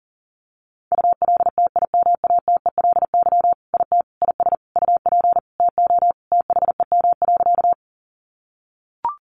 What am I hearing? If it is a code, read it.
Morse code, heard as a keyed tone, about 37 wpm, 700 Hz. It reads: ULTIMATELY IT IS UP TO THEM.
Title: Sentences from Top 1200 Words 40wpm